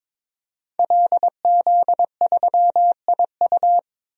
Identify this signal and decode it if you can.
Morse code, heard as a keyed tone, about 22 wpm, 700 Hz. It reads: LZ3IU
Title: DXCC-100 Call Signs - 5 Characters - 22wpm